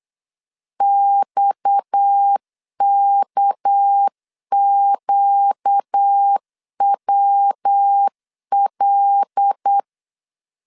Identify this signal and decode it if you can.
Morse code, heard as a keyed tone, about 8 wpm, 800 Hz. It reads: XKQWL
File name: LEZIONI / 10 SIGNS / 10C DOT 8wpm 3649